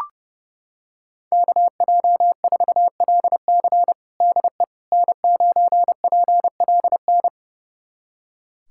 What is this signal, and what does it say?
Morse code, heard as a keyed tone, about 30 wpm, 700 Hz. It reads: KJ4LC DE N9PLN
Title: USA Callsign DE Callsign - 5 Characters - 30wpm